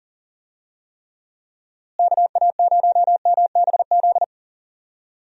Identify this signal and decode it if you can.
Morse code, heard as a keyed tone, about 40 wpm, 700 Hz. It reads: KA0MBZ
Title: USA Call Signs - 6 Characters - Repeated 2x - 40wpm